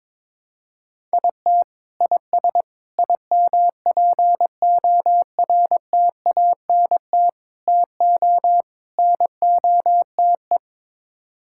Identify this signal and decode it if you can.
Morse code, heard as a keyed tone, about 22 wpm, 700 Hz. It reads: IT IS IMPORTANT TO NOTE
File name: practice / 22wpm / Build A Phrase - 5-Word Phrases - 22wpm